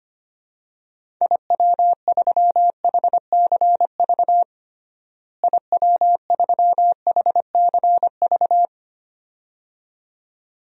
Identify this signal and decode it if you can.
Morse code, heard as a keyed tone, about 25 wpm, 700 Hz. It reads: IW3HCV IW3HCV
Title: DXCC-100 Call Signs - 6 Characters - Repeated 2x - 25wpm